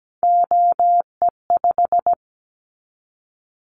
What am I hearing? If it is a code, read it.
Morse code, heard as a keyed tone, about 17 wpm, 700 Hz. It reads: OE5
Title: Callsign Prefixes for Top 25 DXCC - 17wpm